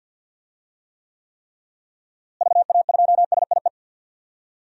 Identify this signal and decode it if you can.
Morse code, heard as a keyed tone, about 49 wpm, 700 Hz. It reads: VA2SIE
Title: DXCC-100 Call Signs - 6 Characters - Repeated 3x - 50wpm